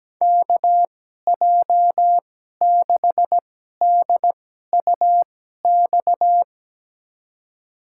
Morse code, heard as a keyed tone, about 17 wpm, spaced as for 12 wpm, 700 Hz. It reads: KJ6DUX
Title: USA Call Signs - 6 Characters - 2x Character Spacing - 17wpm